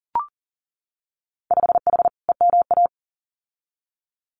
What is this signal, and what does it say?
Morse code, heard as a keyed tone, about 39 wpm, 700 Hz. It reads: 5H EMA
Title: Winter Field Day Exchanges Encore 40wpm